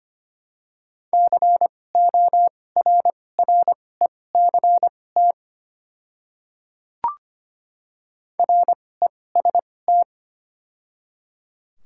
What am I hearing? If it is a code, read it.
Morse code, heard as a keyed tone, about 25 wpm, spaced as for 18 wpm, 700 Hz. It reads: CORRECT REST
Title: 2x Character Spacing - Top 500 Words - 25wpm